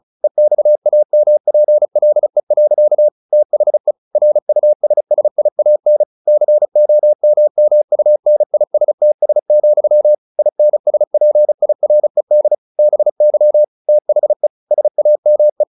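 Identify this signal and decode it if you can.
Morse code, heard as a keyed tone, about 35 wpm, 600 Hz. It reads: EXAMPLE. THE RUSSIAN COMMUNISTS, INSPIRED BY THE SAME
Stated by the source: Morse practice at 35 wpm from CW med Gnister, sending Ken Follett's novel Never in English.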